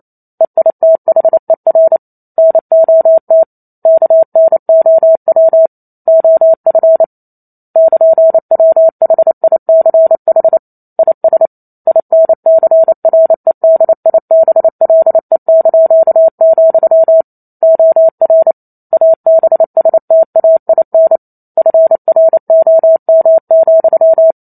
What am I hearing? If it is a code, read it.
Morse code, heard as a keyed tone, about 28 wpm, 650 Hz. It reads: EITHER NOT KNOW OF <KN>WHICH IS INCREDIBLE), OR ABSTAIN FROM,
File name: Chapter 2 - 28 WPM